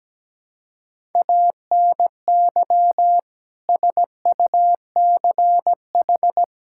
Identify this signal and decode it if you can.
Morse code, heard as a keyed tone, about 17 wpm, 700 Hz. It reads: ANY SUCH
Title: Sentences from Top 1600 Words 17wpm